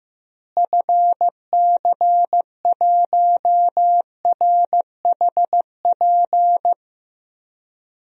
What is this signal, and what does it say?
Morse code, heard as a keyed tone, about 15 wpm, 700 Hz. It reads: FC1RHP
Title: DXCC-100 Call Signs - 6 Characters - Repeated 3x - 15wpm